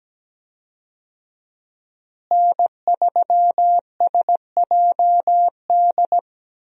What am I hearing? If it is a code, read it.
Morse code, heard as a keyed tone, about 17 wpm, 700 Hz. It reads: N3SJD
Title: USA Call Signs - 5 Characters - Repeated 3x - 17wpm